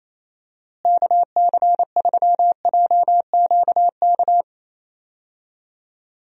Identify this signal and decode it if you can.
Morse code, heard as a keyed tone, about 28 wpm, 700 Hz. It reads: KC3JQK
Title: USA Call Signs - 6 Characters - Repeated 3x - 28wpm